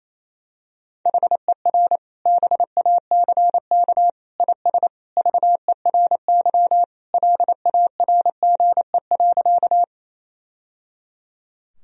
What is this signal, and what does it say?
Morse code, heard as a keyed tone, about 27 wpm, 700 Hz. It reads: HER BACK IS VERY LARGE.
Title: Sentences from Top 1000 Words - Review Part 1 - 28wpm